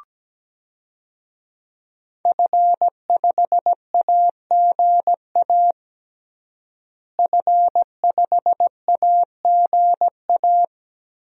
Morse code, heard as a keyed tone, about 17 wpm, 700 Hz. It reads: F5AGA F5AGA
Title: DXCC-100 Call Signs - 5 Characters - Repeated 2x - 17wpm